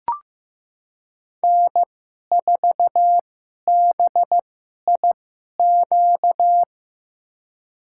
Morse code, heard as a keyed tone, about 15 wpm, spaced as for 11 wpm, 700 Hz. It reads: N4BIQ
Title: USA Call Signs - 5 Characters - 2x Character Spacing - 15wpm